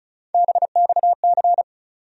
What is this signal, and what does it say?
Morse code, heard as a keyed tone, about 34 wpm, 700 Hz. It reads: BXC
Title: USA Call Sign Postfixes - 3 Letters 35wpm